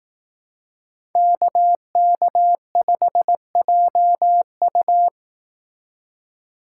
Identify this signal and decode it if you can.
Morse code, heard as a keyed tone, about 18 wpm, 700 Hz. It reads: KK5JU
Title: Speed Racing - USA 2x2 Call Signs 15wpm